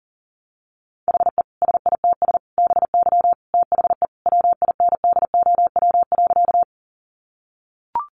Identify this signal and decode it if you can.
Morse code, heard as a keyed tone, about 40 wpm, 700 Hz. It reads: HE SITS BY THE WINDOW.
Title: Easy Phrases 40wpm